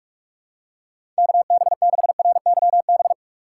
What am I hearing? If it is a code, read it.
Morse code, heard as a keyed tone, about 42 wpm, 700 Hz. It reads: KB6RYB
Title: USA Call Signs - 6 Characters - 4x Character Spacing - 45wpm